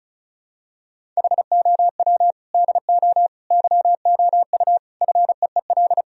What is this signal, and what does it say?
Morse code, heard as a keyed tone, about 34 wpm, 700 Hz. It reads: HOW DO YOU FEEL
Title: Sentences from Top 1000 Words - Review Part 2 - 35wpm